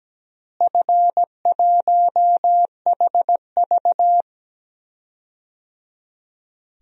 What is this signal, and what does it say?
Morse code, heard as a keyed tone, about 17 wpm, 700 Hz. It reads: F1HV
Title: DXCC-100 Call Signs - 4 Characters - Repeated 3x - 17wpm